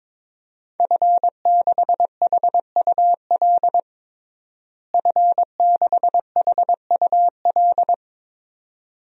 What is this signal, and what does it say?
Morse code, heard as a keyed tone, about 22 wpm, 700 Hz. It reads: F6HUL F6HUL
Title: DXCC-100 Call Signs - 5 Characters - Repeated 2x - 22wpm